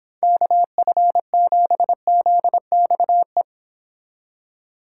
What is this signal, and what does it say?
Morse code, heard as a keyed tone, about 29 wpm, 700 Hz. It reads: KF7ZXE